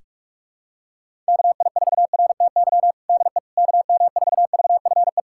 Morse code, heard as a keyed tone, about 46 wpm, 700 Hz. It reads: KI4RTY DE KM4VFE
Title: USA Callsign DE Callsign - 6 Characters - 45wpm